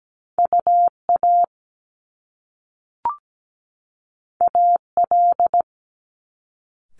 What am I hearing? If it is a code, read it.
Morse code, heard as a keyed tone, about 17 wpm, 700 Hz. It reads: UA AL
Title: Top Two Letter Combinations in Words 17wpm